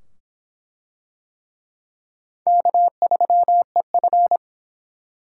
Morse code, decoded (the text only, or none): K3EF